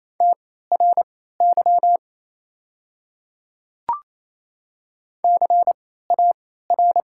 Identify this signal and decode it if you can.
Morse code, heard as a keyed tone, about 28 wpm, spaced as for 16 wpm, 700 Hz. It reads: TRY CAR
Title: 3x Character Spacing - Top 500 Words - 28wpm